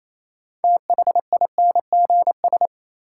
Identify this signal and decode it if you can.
Morse code, heard as a keyed tone, about 28 wpm, 700 Hz. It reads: THINGS
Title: Sentences from Top 1400 Words - New Word Drill Alt - 28wpm